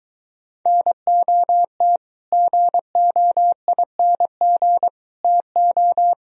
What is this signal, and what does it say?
Morse code, heard as a keyed tone, about 23 wpm, 700 Hz. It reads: NOT GOING TO